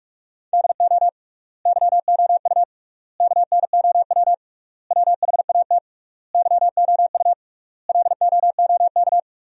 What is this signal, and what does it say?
Morse code, heard as a keyed tone, about 47 wpm, 700 Hz. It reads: DO YOU KNOW WHAT YOU LOOK
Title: 3x Word-Spacing - Sentences from Top 100 Words - 45wpm